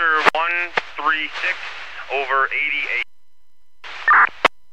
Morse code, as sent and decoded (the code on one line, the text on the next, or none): none